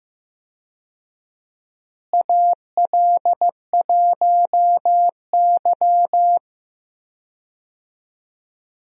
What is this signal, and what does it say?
Morse code, heard as a keyed tone, about 15 wpm, 700 Hz. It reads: AL1Y
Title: USA Call Signs - 4 Characters - Repeated 3x - 15wpm